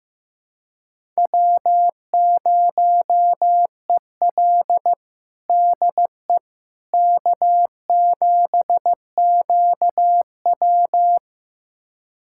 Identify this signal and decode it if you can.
Morse code, heard as a keyed tone, about 15 wpm, 700 Hz. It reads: W0EL DE K7QW
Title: USA Callsign DE Callsign - 4 Characters - 15wpm